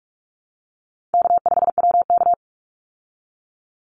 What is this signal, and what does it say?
Morse code, heard as a keyed tone, about 44 wpm, 700 Hz. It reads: K5WK
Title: Easy Call Signs Repeated 2x - 45wpm